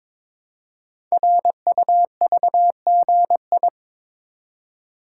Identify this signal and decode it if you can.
Morse code, heard as a keyed tone, about 22 wpm, 700 Hz. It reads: RUVGI
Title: Code Groups - 5 Letter:Numbers - 22wpm